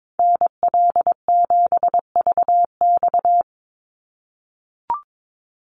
Morse code, heard as a keyed tone, about 22 wpm, 700 Hz. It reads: NL7VX
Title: Call Signs - 22wpm